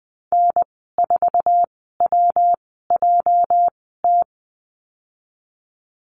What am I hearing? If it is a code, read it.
Morse code, heard as a keyed tone, about 19 wpm, spaced as for 14 wpm, 700 Hz. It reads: N4WJT
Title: 213 Lesson 34g - Number 8 - Call Signs 20wpm